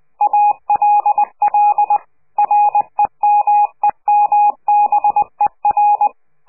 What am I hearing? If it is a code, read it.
Morse code, heard as a keyed tone, about 20 wpm, 830 Hz. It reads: ALL REMEMBER
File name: CQ de PCH laatste bericht farewell